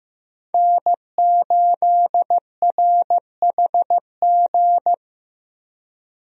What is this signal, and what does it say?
Morse code, heard as a keyed tone, about 15 wpm, 700 Hz. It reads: N8RHG
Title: USA Call Signs - 5 Characters - 15wpm